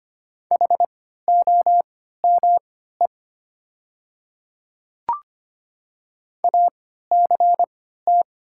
Morse code, heard as a keyed tone, about 25 wpm, spaced as for 14 wpm, 700 Hz. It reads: HOME ACT
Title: 3x Character Spacing - Top 200 Words - 25wpm